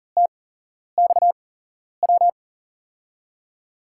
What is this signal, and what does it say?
Morse code, heard as a keyed tone, about 43 wpm, spaced as for 11 wpm, 700 Hz. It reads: TXW
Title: USA Call Signs - 6 Characters - 8x Character Spacing - 40wpm